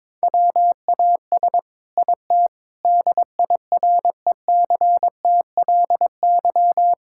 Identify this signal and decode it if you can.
Morse code, heard as a keyed tone, about 21 wpm, 700 Hz. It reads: WAS IT DIRECTLY
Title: Sentences from Top 1000 Words - Review Part 5 - 22wpm